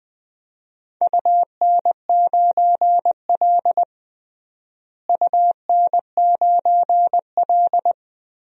Morse code, heard as a keyed tone, about 20 wpm, 700 Hz. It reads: UN9L UN9L
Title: DXCC-100 Call Signs - 4 Characters - Repeated 3x - 20wpm